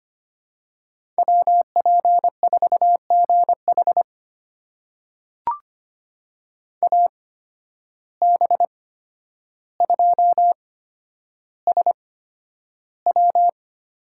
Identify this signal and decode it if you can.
Morse code, heard as a keyed tone, about 25 wpm, spaced as for 11 wpm, 700 Hz. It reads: WP4GH AB2SW
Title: USA Call Signs - 5 Characters - 8x Character Spacing - 25wpm